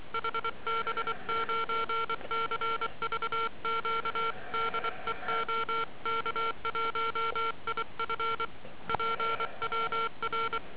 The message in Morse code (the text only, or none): HB9CVQ DE OK1IF PWR